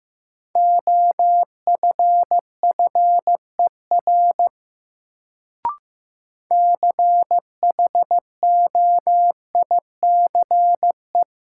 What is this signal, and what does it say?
Morse code, heard as a keyed tone, about 15 wpm, 700 Hz. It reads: OFFER CHOICE
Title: Sentences from Top 900 Words - New Word Drill - 15wpm